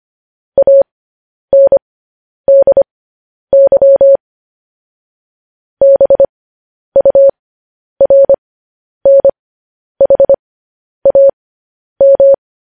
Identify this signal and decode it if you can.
Morse code, heard as a keyed tone, about 25 wpm, spaced as for 10 wpm, 550 Hz. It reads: ANDY BURNHAM
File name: podcast / News headlines-1x-25x10